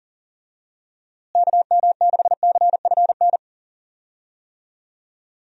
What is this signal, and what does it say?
Morse code, heard as a keyed tone, about 40 wpm, 700 Hz. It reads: KM6CFN
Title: USA Call Signs - 6 Characters - Repeated 2x - 40wpm